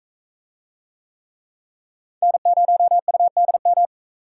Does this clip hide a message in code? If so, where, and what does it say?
Yes: Morse code from 2.2 to 3.9 s, heard as a keyed tone, about 37 wpm, 700 Hz. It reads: N0UDM